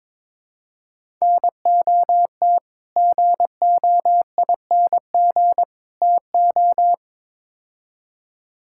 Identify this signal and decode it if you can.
Morse code, heard as a keyed tone, about 22 wpm, 700 Hz. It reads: NOT GOING TO
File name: practice / 22wpm / Build A Phrase - 5-Word Phrases - 22wpm